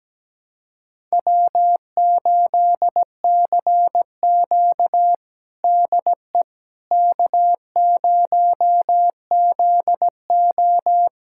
Morse code, heard as a keyed tone, about 17 wpm, 700 Hz. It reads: W8CQ DE K0ZO
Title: USA Callsign DE Callsign - 4 Characters - 17wpm